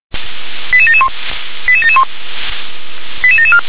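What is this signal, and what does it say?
Five-tone selective call (ZVEI), heard as quick runs of five tones, tones of about 70 ms. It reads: 88771 88771 88774
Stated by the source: Five-tone ZVEI selective call 88702 received by SDR, channel 357.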